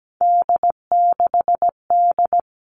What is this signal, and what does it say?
Morse code, heard as a keyed tone, about 17 wpm, 700 Hz. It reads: D6D
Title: All Cut Numbers - 3-Digit 17wpm